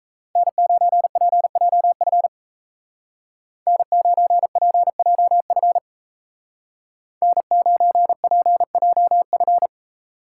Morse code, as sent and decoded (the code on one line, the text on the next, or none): -. ----. .--. .--- ..-. / -. ----. .--. .--- ..-. / -. ----. .--. .--- ..-.
N9PJF N9PJF N9PJF